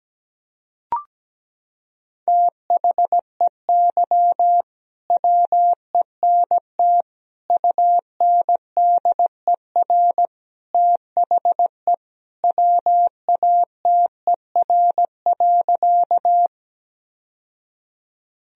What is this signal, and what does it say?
Morse code, heard as a keyed tone, about 17 wpm, 700 Hz. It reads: THEY WENT UNDER THE WATER.
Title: Sentences from Top 1000 Words - Review Part 1 - 17wpm